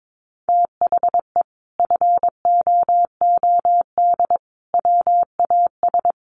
Morse code, heard as a keyed tone, about 22 wpm, 700 Hz. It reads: THE FOOD WAS
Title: Sentences from Top 1400 Words 22wpm